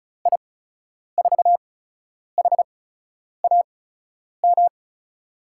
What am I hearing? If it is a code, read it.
Morse code, heard as a keyed tone, about 36 wpm, spaced as for 10 wpm, 700 Hz. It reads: I4HAM
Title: USA Call Signs - 6 Characters - 8x Character Spacing - 35wpm